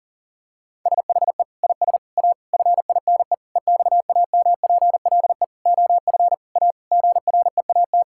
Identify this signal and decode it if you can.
Morse code, heard as a keyed tone, about 40 wpm, 700 Hz. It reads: SHE IS A FINE EXAMPLE OF A GREAT
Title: Sentences from Top 1000 Words - Review Part 6 - 40wpm